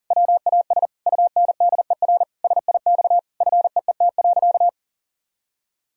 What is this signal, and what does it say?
Morse code, heard as a keyed tone, about 40 wpm, 700 Hz. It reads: WAS UNDER SIX FEET.
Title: Sentences from Top 1000 Words - Review Part 2 - 40wpm